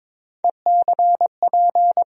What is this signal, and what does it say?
Morse code, heard as a keyed tone, about 21 wpm, 700 Hz. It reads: ECP
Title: USA Call Sign Postfixes - 3 Letters 22wpm